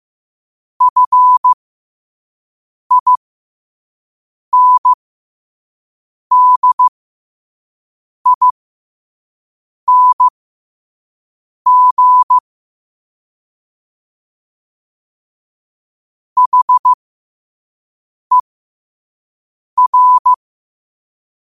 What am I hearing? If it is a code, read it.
Morse code, heard as a keyed tone, about 15 wpm, spaced as for 5 wpm, 1000 Hz. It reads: FINDING HER
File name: qotdfiles / 5wpm